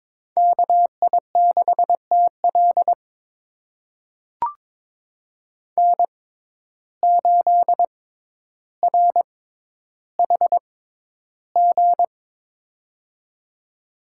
Morse code, heard as a keyed tone, about 22 wpm, spaced as for 11 wpm, 700 Hz. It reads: KI6TL N8RHG
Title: USA Call Signs - 5 Characters - 6x Character Spacing - 22wpm